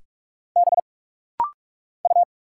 Morse code, heard as a keyed tone, about 44 wpm, 700 Hz. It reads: B U
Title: Single Letter-Number Alt - Rapid-Fire 45wpm